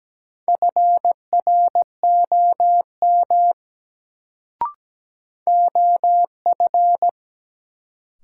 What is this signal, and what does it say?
Morse code, heard as a keyed tone, about 17 wpm, 700 Hz. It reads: FROM OF
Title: Sentences from Top 100 Words - New Word Drill - 17wpm